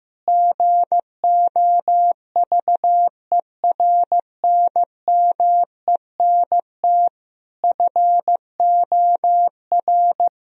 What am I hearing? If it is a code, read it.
Morse code, heard as a keyed tone, about 15 wpm, 700 Hz. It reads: GOVERNMENT FOR
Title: Sentences from Top 1000 Words - Review Part 3 - 15wpm